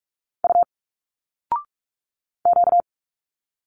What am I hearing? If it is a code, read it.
Morse code, heard as a keyed tone, about 46 wpm, 700 Hz. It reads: U Q
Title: Single Letters 45wpm